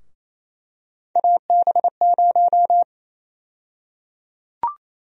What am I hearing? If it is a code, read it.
Morse code, heard as a keyed tone, about 28 wpm, 700 Hz. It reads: AB0
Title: USA Call Sign Prefixes - Repeated 2x - 28wpm